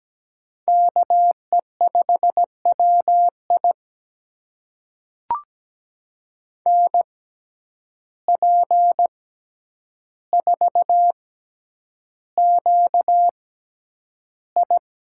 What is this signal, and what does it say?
Morse code, heard as a keyed tone, about 17 wpm, spaced as for 9 wpm, 700 Hz. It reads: KE5WI NP4QI
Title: USA Call Signs - 5 Characters - 6x Character Spacing - 17wpm